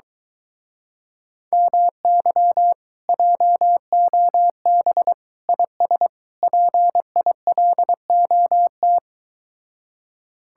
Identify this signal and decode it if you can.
Morse code, heard as a keyed tone, about 23 wpm, 700 Hz. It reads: MY JOB IS PILOT